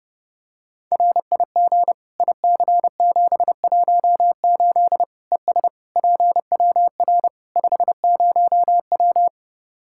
Morse code, heard as a keyed tone, about 30 wpm, 700 Hz. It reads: RIG IC718 ES PWR 50W
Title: Practice QSO - Exchange 2 - Info - 30wpm